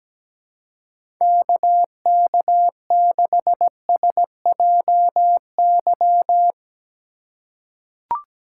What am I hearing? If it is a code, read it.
Morse code, heard as a keyed tone, about 17 wpm, 700 Hz. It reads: KK6SJY
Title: USA Call Signs - 3x Character Spacing - 17wpm